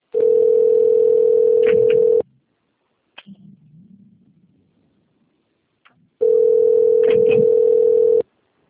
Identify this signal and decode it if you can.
Ringback tone (North America), 440 + 480 Hz.